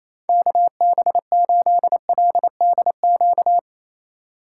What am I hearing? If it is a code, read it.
Morse code, heard as a keyed tone, about 34 wpm, 700 Hz. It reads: KB8LDQ
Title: Speed Racing - USA Call Signs 28wpm